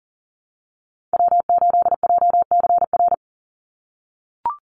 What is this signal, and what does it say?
Morse code, heard as a keyed tone, about 40 wpm, 700 Hz. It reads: W8JCR
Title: USA Call Signs 40wpm